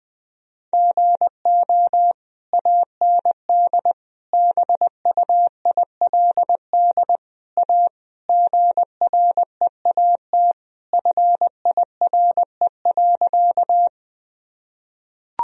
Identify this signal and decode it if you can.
Morse code, heard as a keyed tone, about 20 wpm, 700 Hz. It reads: GO AND BUILD A GREAT FIRE.
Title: Sentences from Top 1000 Words - Review Part 1 - 20wpm